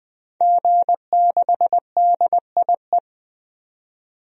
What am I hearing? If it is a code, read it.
Morse code, heard as a keyed tone, about 20 wpm, 700 Hz. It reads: G6DIE
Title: DXCC-100 Call Signs - 5 Characters - 20wpm